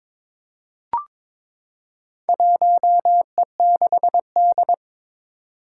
Morse code, heard as a keyed tone, about 22 wpm, 700 Hz. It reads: 1E6D
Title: Four Challenging Letter-Number Combinations 22wpm